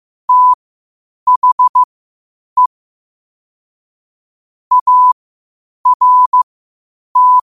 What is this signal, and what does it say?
Morse code, heard as a keyed tone, about 15 wpm, spaced as for 8 wpm, 1000 Hz. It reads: THE ART
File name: qotdfiles / 8wpm